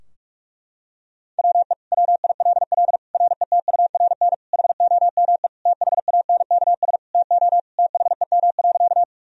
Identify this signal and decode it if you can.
Morse code, heard as a keyed tone, about 42 wpm, 700 Hz. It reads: WE WILL RETURN HOME THANKS TO THEM.
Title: Sentences from Top 1000 Words - Review Part 3 - 45wpm